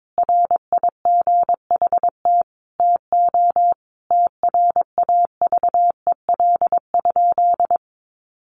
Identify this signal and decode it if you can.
Morse code, heard as a keyed tone, about 22 wpm, 700 Hz. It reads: RIGHT TO TRAVEL?
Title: Sentences from Top 1400 Words 22wpm